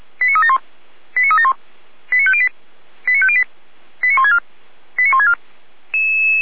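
Five-tone selective call (ZVEI), heard as quick runs of five tones, tones of about 70 ms. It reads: 89371 89371 89508 89508 89164 89164
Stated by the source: ZVEI five-tone call 89169 received by SDR.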